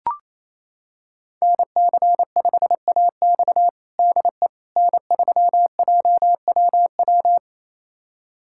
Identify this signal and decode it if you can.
Morse code, heard as a keyed tone, about 27 wpm, 700 Hz. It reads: NC5AX DE N3JWW
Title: USA Callsign DE Callsign - 5 Characters - 28wpm